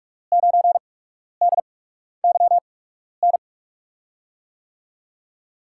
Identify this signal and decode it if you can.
Morse code, heard as a keyed tone, about 43 wpm, spaced as for 12 wpm, 700 Hz. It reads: 9DYN
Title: USA Call Signs - 6 Characters - 8x Character Spacing - 45wpm